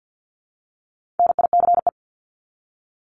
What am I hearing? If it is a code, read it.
Morse code, heard as a keyed tone, about 50 wpm, 700 Hz. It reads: NICE
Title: Top 751-1000 Words 50wpm